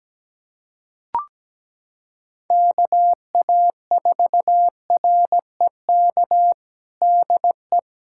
Morse code, heard as a keyed tone, about 17 wpm, 700 Hz. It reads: KA4REK DE
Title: USA Callsign DE Callsign 17wpm